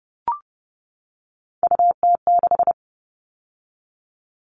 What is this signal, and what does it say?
Morse code, heard as a keyed tone, about 30 wpm, 700 Hz. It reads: UT6
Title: All Cut Numbers - 3-Digit 30wpm